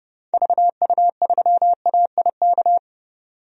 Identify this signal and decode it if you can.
Morse code, heard as a keyed tone, about 30 wpm, 700 Hz. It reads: VU3AIK
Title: DXCC-100 Call Signs - 6 Characters - 30wpm